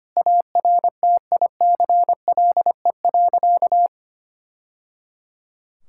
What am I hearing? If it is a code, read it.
Morse code, heard as a keyed tone, about 25 wpm, 700 Hz. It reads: ARTICLE.
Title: Sentences from Top 1000 Words - Review Part 5 - 25wpm